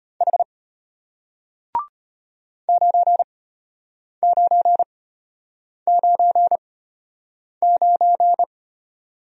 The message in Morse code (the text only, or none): H 9999